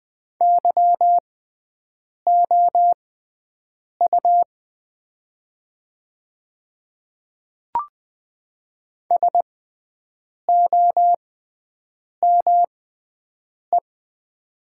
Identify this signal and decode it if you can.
Morse code, heard as a keyed tone, about 20 wpm, spaced as for 7 wpm, 700 Hz. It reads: YOU SOME
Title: 6x Character Spacing - Top 100 Words - 20wpm